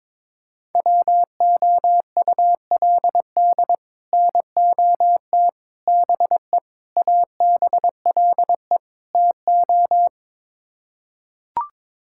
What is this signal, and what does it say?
Morse code, heard as a keyed tone, about 22 wpm, 700 Hz. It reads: WOULD NOT BE ABLE TO
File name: practice / 22wpm / Build A Phrase - 5-Word Phrases - 22wpm